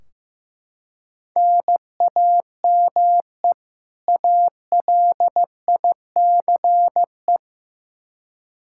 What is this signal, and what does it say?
Morse code, heard as a keyed tone, about 15 wpm, 700 Hz. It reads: NAME ALICE
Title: Practice QSO - Exchange 1 - Beginning - 15wpm